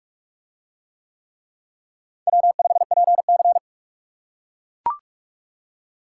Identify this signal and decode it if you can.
Morse code, heard as a keyed tone, about 44 wpm, 700 Hz. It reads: W5PC